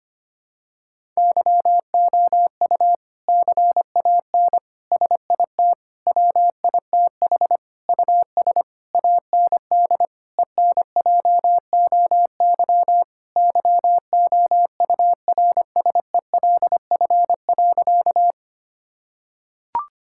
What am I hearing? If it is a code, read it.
Morse code, heard as a keyed tone, about 25 wpm, 700 Hz. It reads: YOU CAN SIT WITH US AND ENJOY YOURSELF.
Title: Sentences from Top 1000 Words - Review Part 5 - 25wpm